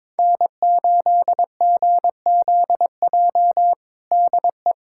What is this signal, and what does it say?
Morse code, heard as a keyed tone, about 22 wpm, 700 Hz. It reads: N8GZJ DE
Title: K1USN SST QSO Practice 22wpm